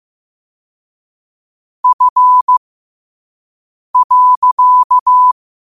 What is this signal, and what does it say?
Morse code, heard as a keyed tone, about 15 wpm, spaced as for 5 wpm, 1000 Hz. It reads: F.
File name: qotdfiles / 5wpm